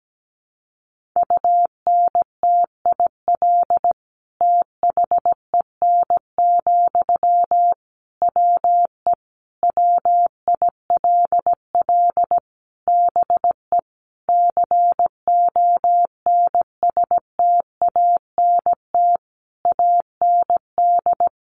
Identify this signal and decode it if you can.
Morse code, heard as a keyed tone, about 17 wpm, 700 Hz. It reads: UNTIL THEN, WE WILL BE CONSTANT AND
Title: Sentences from Top 1500 Words 17wpm